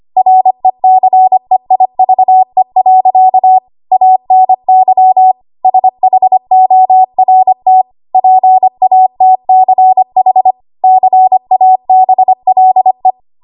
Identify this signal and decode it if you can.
Morse code, heard as a keyed tone, about 25 wpm, 750 Hz. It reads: RECEIVE. ANY SHORT PATCH CABLE
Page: W1AW Web Code Practice Files